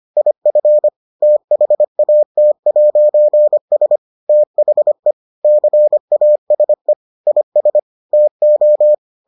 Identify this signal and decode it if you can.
Morse code, heard as a keyed tone, about 25 wpm, 600 Hz. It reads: IF THAT'S THE CASE IS TO